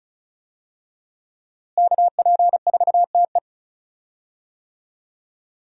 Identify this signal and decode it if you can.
Morse code, heard as a keyed tone, about 36 wpm, 700 Hz. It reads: KP4TE
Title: DXCC-100 Call Signs - 5 Characters - Repeated 2x - 35wpm